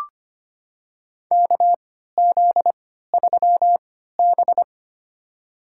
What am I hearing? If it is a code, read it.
Morse code, heard as a keyed tone, about 25 wpm, spaced as for 14 wpm, 700 Hz. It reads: KZ3B
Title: USA Call Signs - 4 Characters - 3x Character Spacing - 25wpm